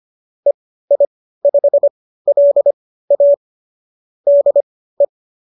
Morse code, heard as a keyed tone, about 25 wpm, spaced as for 15 wpm, 590 Hz. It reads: EI5LA DE